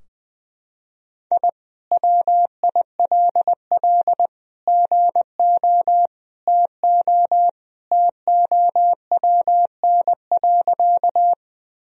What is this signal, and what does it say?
Morse code, heard as a keyed tone, about 21 wpm, 700 Hz. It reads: I WILL GO TO TOWN.
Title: Sentences from Top 1000 Words - Review Part 2 - 20wpm